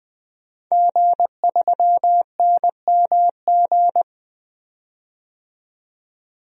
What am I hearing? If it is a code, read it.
Morse code, heard as a keyed tone, about 20 wpm, 700 Hz. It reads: G3NMG
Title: DXCC-100 Call Signs - 5 Characters - 20wpm